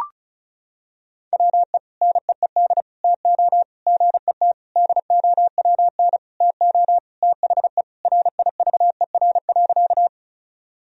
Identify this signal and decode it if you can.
Morse code, heard as a keyed tone, about 34 wpm, 700 Hz. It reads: WE NEED TO GET DOWN TO THE RIVER.
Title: Sentences from Top 1000 Words - Review Part 1 - 35wpm